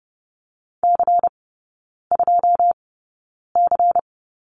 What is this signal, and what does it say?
Morse code, heard as a keyed tone, about 30 wpm, spaced as for 9 wpm, 700 Hz. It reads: C2C
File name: Lesson 16e - Letter c - Three Characters 30wpm